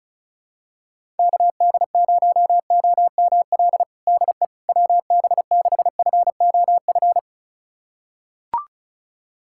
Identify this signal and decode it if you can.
Morse code, heard as a keyed tone, about 36 wpm, 700 Hz. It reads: KD0OML DE WB6FOF
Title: USA Callsign DE Callsign 35wpm